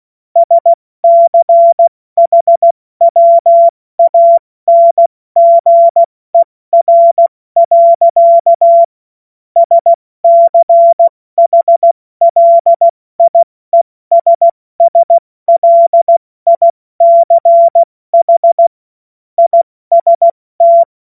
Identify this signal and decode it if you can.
Morse code, heard as a keyed tone, about 16 wpm, 680 Hz. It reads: SCHWANGER. SCHLIESSLICH IST
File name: Morsekurs / Witze / Witz 27 Mai 2025 70 F80